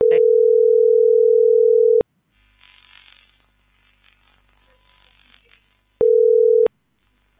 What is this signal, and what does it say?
Ringback tone (North America), 440 + 480 Hz.